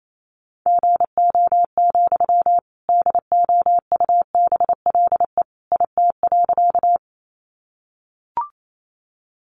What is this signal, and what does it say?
Morse code, heard as a keyed tone, about 28 wpm, 700 Hz. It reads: GO, DOUBLE IT.
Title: Sentences from Top 1400 Words 28wpm